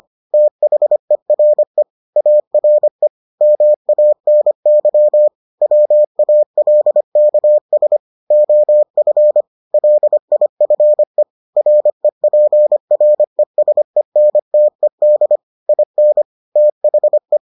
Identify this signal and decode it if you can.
Morse code, heard as a keyed tone, about 25 wpm, 600 Hz. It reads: THERE ARE MANY WALKS OF LIFE REPRESENTED IN THE